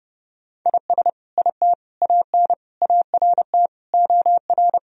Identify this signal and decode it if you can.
Morse code, heard as a keyed tone, about 32 wpm, 700 Hz. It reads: IS IT AN ART OR
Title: Sentences from Top 1000 Words - Review Part 3 - 30wpm